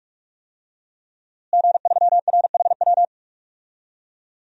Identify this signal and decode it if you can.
Morse code, heard as a keyed tone, about 41 wpm, 700 Hz. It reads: G3RHW
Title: DXCC-100 Call Signs - 5 Characters - Repeated 3x - 45wpm